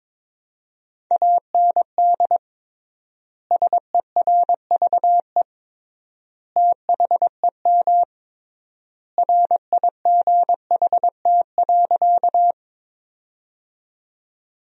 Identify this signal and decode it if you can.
Morse code, heard as a keyed tone, about 21 wpm, 700 Hz. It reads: AND SERVE THEM RIGHT.
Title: 3x Word-Spacing - Sentences from Top 400 Words - 22wpm